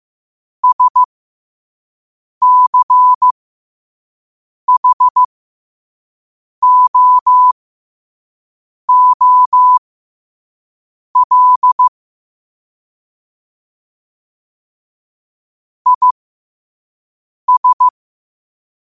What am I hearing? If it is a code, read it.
Morse code, heard as a keyed tone, about 15 wpm, spaced as for 5 wpm, 1000 Hz. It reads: SCHOOL IS
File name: qotdfiles / 5wpm